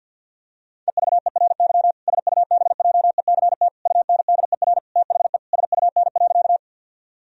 Morse code, heard as a keyed tone, about 51 wpm, 700 Hz. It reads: EVERY SUBJECT UNDER THE SUN.
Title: Sentences from Top 1000 Words - Review Part 1 - 50wpm